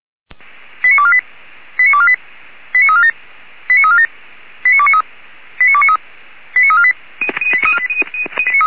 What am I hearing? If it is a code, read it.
Five-tone selective call (ZVEI), heard as quick runs of five tones, tones of about 70 ms. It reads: 89248 89248 89347 89347 89293 89293 89348